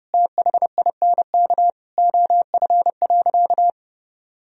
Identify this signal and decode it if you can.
Morse code, heard as a keyed tone, about 30 wpm, 700 Hz. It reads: THINK OF.
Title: Sentences from Top 1000 Words - Review Part 1 - 30wpm